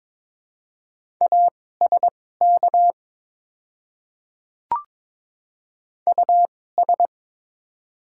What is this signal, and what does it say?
Morse code, heard as a keyed tone, about 22 wpm, spaced as for 16 wpm, 700 Hz. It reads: ASK US
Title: 2x Character Spacing - Top 300 Words - 22wpm